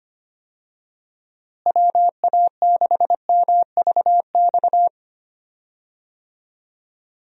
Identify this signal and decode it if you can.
Morse code, heard as a keyed tone, about 25 wpm, 700 Hz. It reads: WA6MVX
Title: USA Call Signs - 6 Characters - Repeated 2x - 25wpm